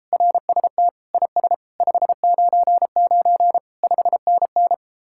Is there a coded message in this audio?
Yes: Morse code from 0.1 to 4.7 s, heard as a keyed tone, about 33 wpm, 700 Hz. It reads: RST IS 599 5NN